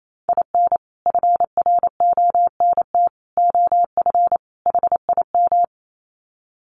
Morse code, heard as a keyed tone, about 28 wpm, 700 Hz. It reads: IN FRONT OF HIM
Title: Sets of 4 Words 28wpm